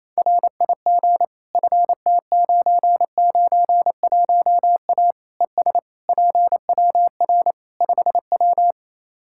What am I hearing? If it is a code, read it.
Morse code, heard as a keyed tone, about 28 wpm, 700 Hz. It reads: RIG FT991A ES PWR 5W
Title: Practice QSO - Exchange 2 - Info - 28wpm